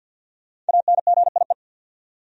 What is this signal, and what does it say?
Morse code, heard as a keyed tone, about 49 wpm, 700 Hz. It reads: ANGIE
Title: USA Common Names - Max 5 Letters - 50wpm